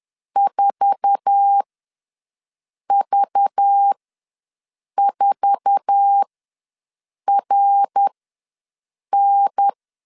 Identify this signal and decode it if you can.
Morse code, heard as a keyed tone, about 10 wpm, spaced as for 6 wpm, 800 Hz. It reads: 4V4RN
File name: LEZIONI / 8 YJ45 / 8 YJ4 ws 10wpm 4020